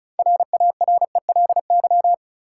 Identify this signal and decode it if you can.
Morse code, heard as a keyed tone, about 36 wpm, 700 Hz. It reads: RARELY
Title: Six Letter Words - 35wpm